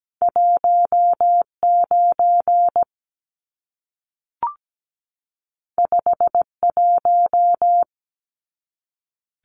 Morse code, heard as a keyed tone, about 17 wpm, 700 Hz. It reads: 19 51
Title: Numbers - 2-Digit 17wpm